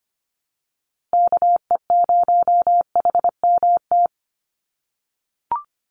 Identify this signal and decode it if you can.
Morse code, heard as a keyed tone, about 25 wpm, 700 Hz. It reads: KE0HMT
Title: USA Call Signs Repeated 2x 25wpm